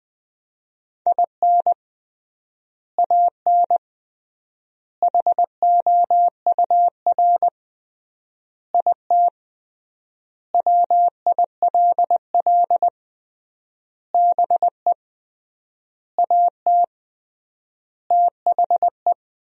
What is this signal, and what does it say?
Morse code, heard as a keyed tone, about 20 wpm, 700 Hz. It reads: IN AN HOUR IT WILL BE AT THE
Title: 3x Word-Spacing - Sentences from Top 400 Words - 20wpm